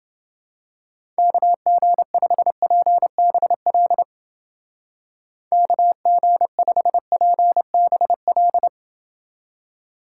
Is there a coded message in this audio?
Yes: Morse code from 1.2 to 8.7 s, heard as a keyed tone, about 29 wpm, 700 Hz. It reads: KG5PBL KG5PBL